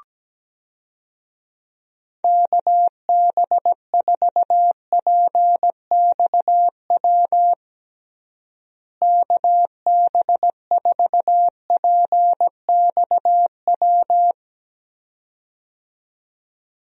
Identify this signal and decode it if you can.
Morse code, heard as a keyed tone, about 17 wpm, 700 Hz. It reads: KB4PXW KB4PXW
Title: USA Call Signs - 6 Characters - Repeated 2x - 17wpm